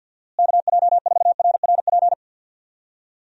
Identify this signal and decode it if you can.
Morse code, heard as a keyed tone, about 52 wpm, 700 Hz. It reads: KJ4RRP
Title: USA Call Signs - 6 Characters - Repeated 2x - 50wpm